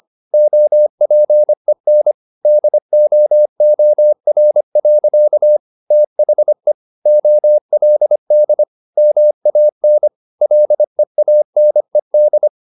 Morse code, heard as a keyed tone, about 25 wpm, 600 Hz. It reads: OPEN DOOR. THE OLD MAN LEANED
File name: the old man and the sea part 1 25wpm